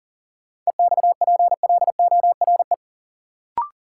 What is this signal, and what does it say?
Morse code, heard as a keyed tone, about 40 wpm, 700 Hz. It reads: EXPLORE
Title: Sentences from Top 1100 Words - New Word Drill - 40wpm